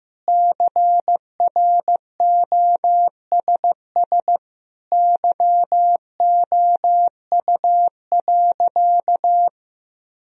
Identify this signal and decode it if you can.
Morse code, heard as a keyed tone, about 15 wpm, 700 Hz. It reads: CROSS YOU.
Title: Sentences from Top 1000 Words - Review Part 1 - 15wpm